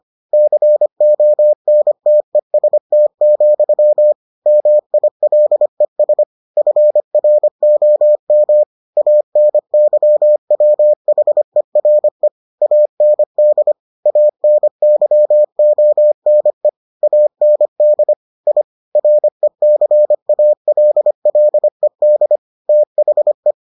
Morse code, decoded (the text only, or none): CONTEST, MILES FROM ANYWHERE AND ANYONE AND I RECALLED THE